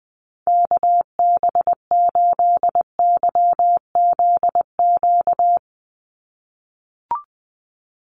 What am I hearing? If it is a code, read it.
Morse code, heard as a keyed tone, about 20 wpm, 700 Hz. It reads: KB8YZQ
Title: USA Call Signs 20wpm